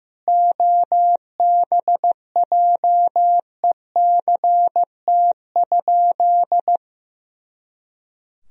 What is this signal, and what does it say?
Morse code, heard as a keyed tone, about 15 wpm, 700 Hz. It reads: OBJECT?
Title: Sentences from Top 500 Words and Abbreviations 15wpm